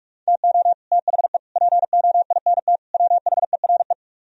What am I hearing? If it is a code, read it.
Morse code, heard as a keyed tone, about 40 wpm, 700 Hz. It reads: TO THE POINT WHERE